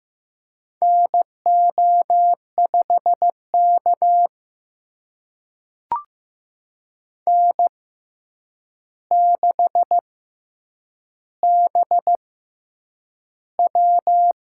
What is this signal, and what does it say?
Morse code, heard as a keyed tone, about 15 wpm, spaced as for 8 wpm, 700 Hz. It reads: NO5K N6BW
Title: USA Call Signs - 4 Characters - 6x Character Spacing - 15wpm